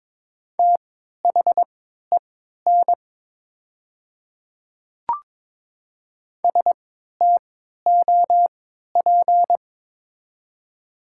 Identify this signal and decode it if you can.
Morse code, heard as a keyed tone, about 22 wpm, spaced as for 12 wpm, 700 Hz. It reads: THEN STOP
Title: 3x Character Spacing - Top 500 Words - 22wpm